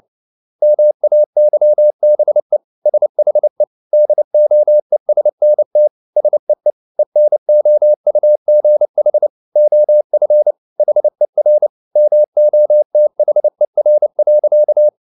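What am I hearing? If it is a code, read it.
Morse code, heard as a keyed tone, about 29 wpm, 600 Hz. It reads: MAYBE SHE DOESNT SEE ENOUGH OF HER MOTHER.
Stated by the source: Morse code at 29 wpm, sending Ken Follett's novel Never in English, from CW med Gnister.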